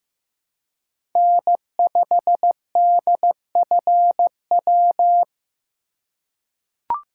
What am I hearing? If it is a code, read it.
Morse code, heard as a keyed tone, about 15 wpm, 700 Hz. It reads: N5DFW
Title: USA Call Signs - 5 Characters - 3x Character Spacing - 15wpm